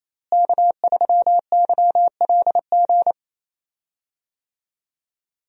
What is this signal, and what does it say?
Morse code, heard as a keyed tone, about 28 wpm, 700 Hz. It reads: K3YLG
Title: USA Call Signs - 5 Characters - Repeated 2x - 28wpm